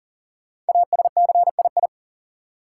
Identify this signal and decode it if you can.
Morse code, heard as a keyed tone, about 40 wpm, 700 Hz. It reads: ASCII
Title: Sets of 5 Words - New Word Drill - 40wpm